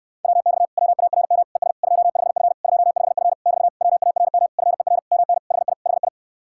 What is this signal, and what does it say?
Morse code, heard as a keyed tone, about 65 wpm, 700 Hz. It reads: WX WARM ES 25F 25F <BT> WARM FER MN HI HI